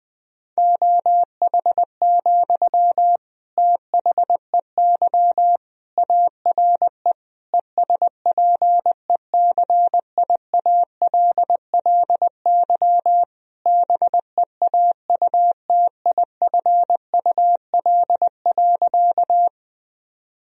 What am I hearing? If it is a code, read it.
Morse code, heard as a keyed tone, about 20 wpm, 700 Hz. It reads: OH, THEY ARE ESPECIALLY BEAUTIFUL.
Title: Sentences from Top 1000 Words - Review Part 4 - 20wpm